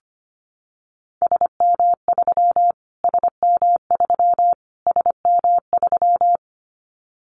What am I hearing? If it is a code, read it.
Morse code, heard as a keyed tone, about 25 wpm, 700 Hz. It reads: SM3 SM3 SM3
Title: Callsign Prefixes for Top 25 DXCC - Repeated 3x - 25wpm